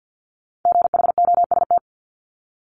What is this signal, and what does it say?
Morse code, heard as a keyed tone, about 50 wpm, 690 Hz. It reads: GHOST
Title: Multi-letter Phonemes Repeated 3x 50wpm